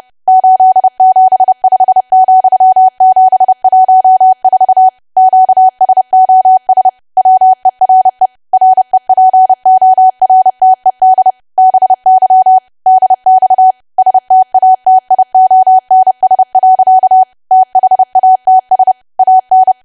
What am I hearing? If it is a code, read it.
Morse code, heard as a keyed tone, about 30 wpm, 750 Hz. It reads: 875,714 QSOS WERE REPORTED BY DX STATIONS. THATS AN
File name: cwa / W1AW-Level3 / 30WPM